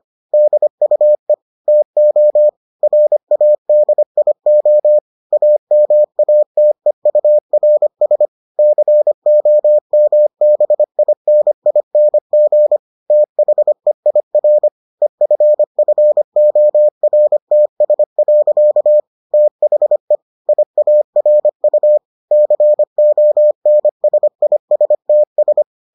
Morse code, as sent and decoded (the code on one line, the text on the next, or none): -.. ..- . / - --- / .-. .- -.. .. --- / .- -- .- - . ..- .-. ... / -.-. --- -- -... .. -. .. -. --. / - .... .. .-. / . ..-. ..-. --- .-. - ... .-.-.- / - .... / .. .- .-. ..- / -.-. --- -. ... .. ... - ...
DUE TO RADIO AMATEURS COMBINING THEIR EFFORTS. THE IARU CONSISTS